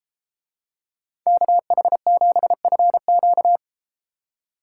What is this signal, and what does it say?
Morse code, heard as a keyed tone, about 33 wpm, 700 Hz. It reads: KH7FQ